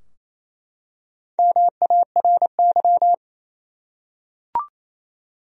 Morse code, heard as a keyed tone, about 28 wpm, 700 Hz. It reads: MARY
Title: USA Common Names - 28wpm